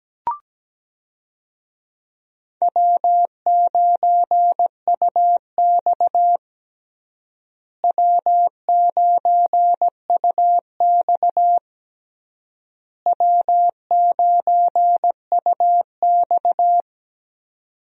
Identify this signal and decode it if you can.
Morse code, heard as a keyed tone, about 17 wpm, 700 Hz. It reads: W9UX W9UX W9UX
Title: USA Call Signs - 4 Characters - Repeated 3x - 17wpm